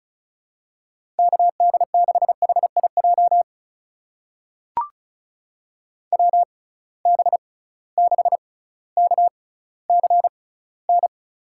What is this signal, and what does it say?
Morse code, heard as a keyed tone, about 34 wpm, spaced as for 18 wpm, 700 Hz. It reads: KD6HIJ WB6KCN
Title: USA Call Signs - 6 Characters - 6x Character Spacing - 35wpm